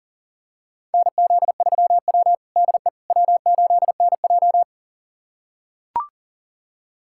Morse code, heard as a keyed tone, about 40 wpm, 700 Hz. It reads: NZ3W DE W8NJ
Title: USA Callsign DE Callsign - 4 Characters - 40wpm